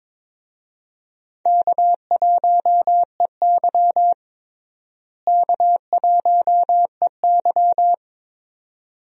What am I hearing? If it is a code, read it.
Morse code, heard as a keyed tone, about 22 wpm, 700 Hz. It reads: K1EY K1EY
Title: USA Call Signs - 4 Characters - Repeated 2x - 22wpm